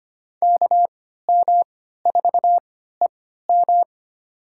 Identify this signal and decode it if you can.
Morse code, heard as a keyed tone, about 25 wpm, spaced as for 14 wpm, 700 Hz. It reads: KM4EM
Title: USA Call Signs - 5 Characters - 3x Character Spacing - 25wpm